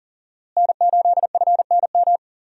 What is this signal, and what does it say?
Morse code, heard as a keyed tone, about 40 wpm, 700 Hz. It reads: N8FNM
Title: USA Call Signs - 5 Characters - Repeated 3x - 40wpm